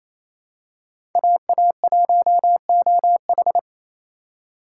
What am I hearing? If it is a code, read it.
Morse code, heard as a keyed tone, about 28 wpm, 700 Hz. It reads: AA1OH
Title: USA Call Signs - 8x Character Spacing - 28wpm